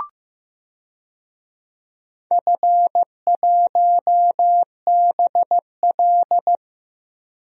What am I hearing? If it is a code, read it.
Morse code, heard as a keyed tone, about 15 wpm, 700 Hz. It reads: F1BL
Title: DXCC-100 Call Signs - 4 Characters - 15wpm